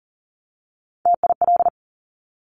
Morse code, heard as a keyed tone, about 41 wpm, 700 Hz. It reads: TIL
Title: Top Three Letter Combinations in Words 40wpm